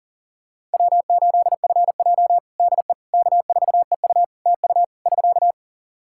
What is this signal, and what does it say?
Morse code, heard as a keyed tone, about 37 wpm, 700 Hz. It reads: W8FJ DE K4EU TU <SK>
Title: Practice QSO - Exchange 4 - Ending - 40wpm